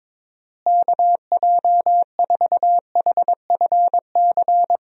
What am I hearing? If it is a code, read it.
Morse code, heard as a keyed tone, about 22 wpm, 700 Hz. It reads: KJ4HFC
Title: USA Call Signs - 6 Characters - Repeated 3x - 22wpm